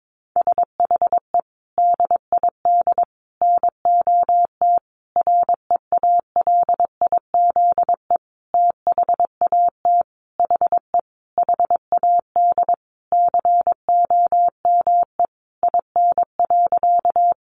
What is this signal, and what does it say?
Morse code, heard as a keyed tone, about 22 wpm, 700 Hz. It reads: SHE DID NOT REALIZE THAT HE HAD COME IN.
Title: Sentences from Top 1000 Words 22wpm